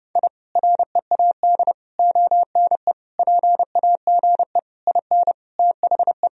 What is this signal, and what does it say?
Morse code, heard as a keyed tone, about 30 wpm, 700 Hz. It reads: I READ ONE PAGE IN THE
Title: Sentences from Top 1000 Words - Review Part 1 - 30wpm